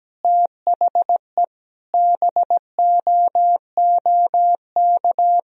Morse code, heard as a keyed tone, about 17 wpm, 700 Hz. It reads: THE BOOK
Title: Sentences from Top 1000 Words - Review Part 3 - 17wpm